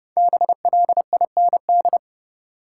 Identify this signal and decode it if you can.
Morse code, heard as a keyed tone, about 30 wpm, 700 Hz. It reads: BLIND
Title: Five Letter Words - 30wpm